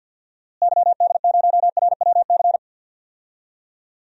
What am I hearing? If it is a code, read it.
Morse code, heard as a keyed tone, about 46 wpm, 700 Hz. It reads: YD0RWC
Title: DXCC-100 Call Signs - 6 Characters - Repeated 2x - 50wpm